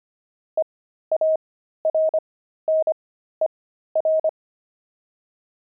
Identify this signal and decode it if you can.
Morse code, heard as a keyed tone, about 25 wpm, spaced as for 13 wpm, 630 Hz. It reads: EARNER
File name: pangram-fs-13-cs-25-zeal